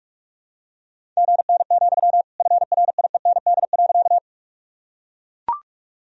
Morse code, heard as a keyed tone, about 41 wpm, 700 Hz. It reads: GN, FRIEND.
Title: Sentences from Top 300 Words and Abbreviations 45wpm